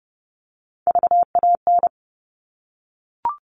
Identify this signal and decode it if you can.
Morse code, heard as a keyed tone, about 30 wpm, 700 Hz. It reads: VAN
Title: All Cut Numbers - 3-Digit 30wpm